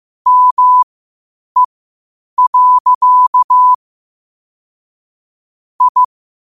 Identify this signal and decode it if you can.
Morse code, heard as a keyed tone, about 15 wpm, spaced as for 8 wpm, 1000 Hz. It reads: ME. I
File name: qotdfiles / 8wpm